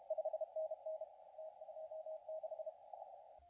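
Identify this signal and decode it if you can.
Morse code, heard as a keyed tone, about 30 wpm, 630 Hz, with radio noise. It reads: 5NN T26
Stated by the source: CW contest audio recorded at LA4C during the SAC CW contest.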